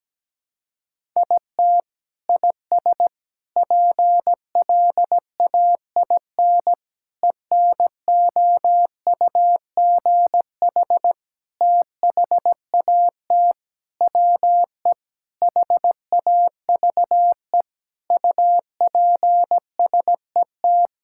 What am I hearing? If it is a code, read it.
Morse code, heard as a keyed tone, about 17 wpm, 700 Hz. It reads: IT IS PLAIN ENOUGH THAT WE HAVE UPSET
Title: Sentences from Top 1000 Words - Review Part 5 - 17wpm